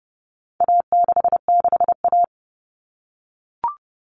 Morse code, heard as a keyed tone, about 30 wpm, 700 Hz. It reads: A66A
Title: All Cut Numbers - 4-Digit 30wpm